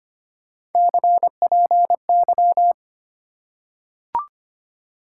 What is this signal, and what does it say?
Morse code, heard as a keyed tone, about 25 wpm, 700 Hz. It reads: CPY
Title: Top 90 QSO Elements 25wpm